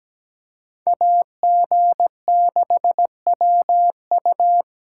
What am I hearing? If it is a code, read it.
Morse code, heard as a keyed tone, about 20 wpm, 700 Hz. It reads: AG6WU
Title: Speed Racing - USA Call Signs 17wpm